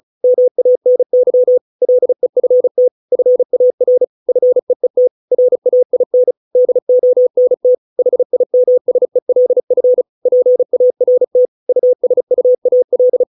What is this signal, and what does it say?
Morse code, heard as a keyed tone, about 35 wpm, 500 Hz. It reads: MANY LEFT FAR FEET RAIN DONT HIMSELF PART USUAL